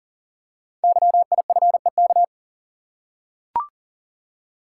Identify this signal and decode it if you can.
Morse code, heard as a keyed tone, about 42 wpm, 690 Hz. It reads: YIFEK
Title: Code Groups - 5 Letters - 40wpm